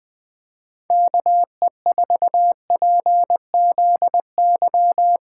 Morse code, heard as a keyed tone, about 20 wpm, 700 Hz. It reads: KE4PZY
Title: Speed Racing - USA 2x3 Call Signs 17wpm